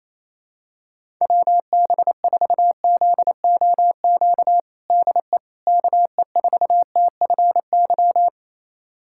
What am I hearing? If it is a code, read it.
Morse code, heard as a keyed tone, about 28 wpm, 700 Hz. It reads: WB4ZOQ DE KE4TFY
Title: USA Callsign DE Callsign 28wpm